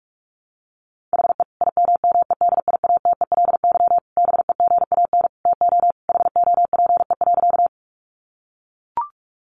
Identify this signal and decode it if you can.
Morse code, heard as a keyed tone, about 44 wpm, 700 Hz. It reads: HE IMMEDIATELY BEGAN TO HOPE.
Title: Sentences from Top 1100 Words 45wpm